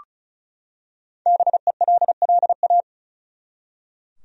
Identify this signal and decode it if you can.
Morse code, heard as a keyed tone, about 36 wpm, 700 Hz. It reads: BELLA
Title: USA Common Names - 35wpm